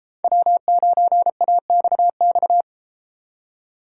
Morse code, heard as a keyed tone, about 30 wpm, 700 Hz. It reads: W9AXX